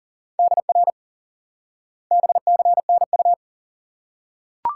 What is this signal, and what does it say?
Morse code, heard as a keyed tone, about 37 wpm, 700 Hz. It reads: DR BCNU
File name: IQE Course 3 - Lesson 89 - YAGI - Two Elements - 40wpm